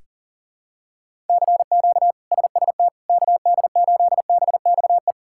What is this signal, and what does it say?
Morse code, heard as a keyed tone, about 40 wpm, 690 Hz. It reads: CQ SST KD8BXE
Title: K1USN SST QSO Practice - 40wpm